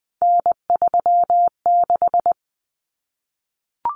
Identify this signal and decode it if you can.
Morse code, heard as a keyed tone, about 20 wpm, 700 Hz. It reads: N36
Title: Common Cut Numbers - 3-Digit 20wpm